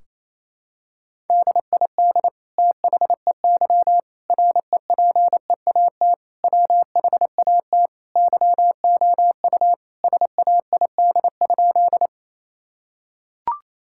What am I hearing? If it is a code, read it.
Morse code, heard as a keyed tone, about 28 wpm, 700 Hz. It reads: DID THEY REPEAT WHAT YOU SAID?
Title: Sentences from Top 1600 Words 28wpm